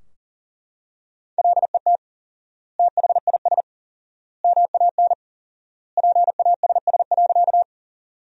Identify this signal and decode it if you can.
Morse code, heard as a keyed tone, about 37 wpm, 700 Hz. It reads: LET THIS MAN PASS.
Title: 4x Word-Spacing - Sentences from Top 400 Words - 40wpm